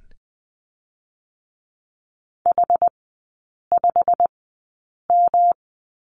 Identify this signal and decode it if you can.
Morse code, heard as a keyed tone, about 20 wpm, spaced as for 8 wpm, 700 Hz. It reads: H5M
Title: 114 Lesson 19e - Letter w - Three Characters 20wpm